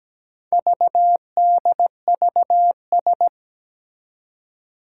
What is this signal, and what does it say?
Morse code, heard as a keyed tone, about 17 wpm, 700 Hz. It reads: VDVS
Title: Four Challenging Letter-Number Combinations 17wpm